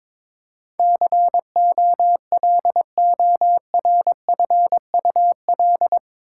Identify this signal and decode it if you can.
Morse code, heard as a keyed tone, about 21 wpm, 700 Hz. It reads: COLORFUL
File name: practice / 22wpm / Eight Letter Words - 22wpm